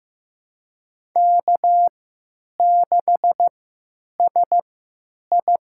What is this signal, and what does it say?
Morse code, heard as a keyed tone, about 15 wpm, spaced as for 9 wpm, 700 Hz. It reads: K6SI
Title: USA Call Signs - 3x Character Spacing - 15wpm